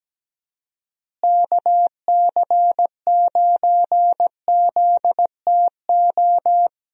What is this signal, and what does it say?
Morse code, heard as a keyed tone, about 17 wpm, 700 Hz. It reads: KC9ZTO
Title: USA Call Signs - 6 Characters - Repeated 3x - 17wpm